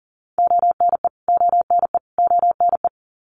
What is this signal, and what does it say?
Morse code, heard as a keyed tone, about 38 wpm, 700 Hz. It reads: ONE ONE ONE
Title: Top 300 Words Repeated 40wpm